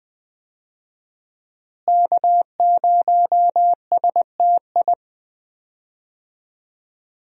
Morse code, heard as a keyed tone, about 20 wpm, 700 Hz. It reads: K0STI
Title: USA Call Signs - 5 Characters - Repeated 2x - 20wpm